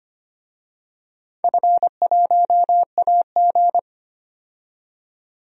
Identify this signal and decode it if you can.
Morse code, heard as a keyed tone, about 25 wpm, 700 Hz. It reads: F1AG
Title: DXCC-100 Call Signs - 4 Characters - 25wpm